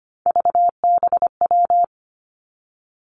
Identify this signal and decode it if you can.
Morse code, heard as a keyed tone, about 25 wpm, 700 Hz. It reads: VBW
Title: Three Challenging Letter-Number Combinations 25wpm